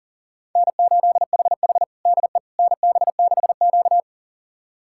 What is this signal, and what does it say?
Morse code, heard as a keyed tone, about 40 wpm, 700 Hz. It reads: N8HH DE NB6Q
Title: USA Callsign DE Callsign - 4 Characters - 40wpm